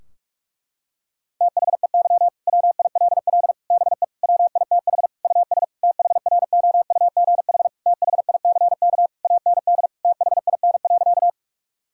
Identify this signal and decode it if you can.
Morse code, heard as a keyed tone, about 46 wpm, 700 Hz. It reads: THEY WILL BE WITH US THROUGH THICK AND THIN.
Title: Sentences from Top 1600 Words 45wpm